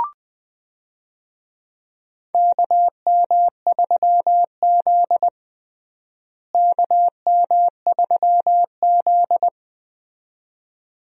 Morse code, heard as a keyed tone, about 20 wpm, 700 Hz. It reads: KM3Z KM3Z
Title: USA Call Signs - 4 Characters - Repeated 2x - 20wpm